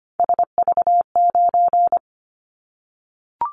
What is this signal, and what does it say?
Morse code, heard as a keyed tone, about 25 wpm, 700 Hz. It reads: SV9
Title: Callsign Prefixes for Top 100 DXCC - 25wpm